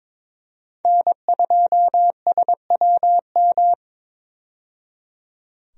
Morse code, heard as a keyed tone, about 22 wpm, 700 Hz. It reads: N2SWM
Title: USA Call Signs - 5 Characters - 22wpm